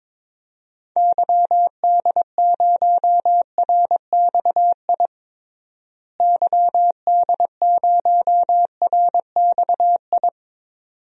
Morse code, heard as a keyed tone, about 22 wpm, 700 Hz. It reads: YD0RXI YD0RXI
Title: DXCC-100 Call Signs - 6 Characters - Repeated 2x - 22wpm